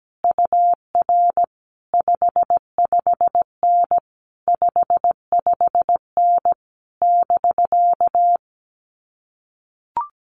Morse code, heard as a keyed tone, about 17 wpm, 700 Hz. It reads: UR 55N 55N <BK>
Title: Common QSO Phrases Encore 17wpm